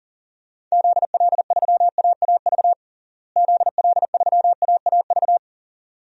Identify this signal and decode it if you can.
Morse code, heard as a keyed tone, about 40 wpm, 700 Hz. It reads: ZL3AAV ZL3AAV
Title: DXCC-100 Call Signs - 6 Characters - Repeated 2x - 40wpm